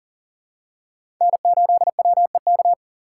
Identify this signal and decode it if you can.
Morse code, heard as a keyed tone, about 40 wpm, 700 Hz. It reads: N8WEK
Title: USA Call Signs - 5 Characters - 3x Character Spacing - 40wpm